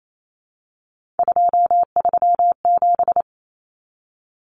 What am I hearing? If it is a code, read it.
Morse code, heard as a keyed tone, about 28 wpm, 700 Hz. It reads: 237
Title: Numbers - 3-Digit 28wpm